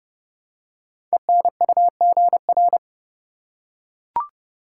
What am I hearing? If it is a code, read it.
Morse code, heard as a keyed tone, about 30 wpm, 700 Hz. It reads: ENUGR
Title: Code Groups - 5 Letters - 30wpm